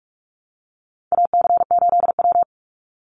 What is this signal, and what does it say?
Morse code, heard as a keyed tone, about 44 wpm, 700 Hz. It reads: AC8W
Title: Call Signs Repeated 3x - 45wpm